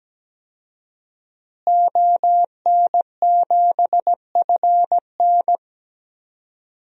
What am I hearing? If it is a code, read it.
Morse code, heard as a keyed tone, about 17 wpm, 700 Hz. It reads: ON7FN
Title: DXCC-100 Call Signs - 5 Characters - Repeated 2x - 17wpm